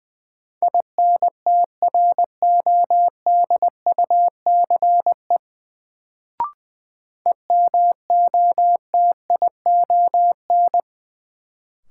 Morse code, heard as a keyed tone, about 20 wpm, 700 Hz. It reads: INTRODUCE EMOTION
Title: Sentences from Top 1600 Words - New Word Drill - 20wpm